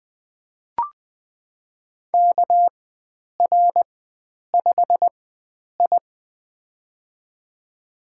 Morse code, heard as a keyed tone, about 21 wpm, spaced as for 9 wpm, 700 Hz. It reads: KR5I
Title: USA Call Signs - 4x Character Spacing - 20wpm